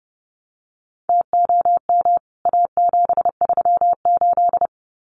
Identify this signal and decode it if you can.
Morse code, heard as a keyed tone, about 30 wpm, 700 Hz. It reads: TOM A738
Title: CWT Exchange 30wpm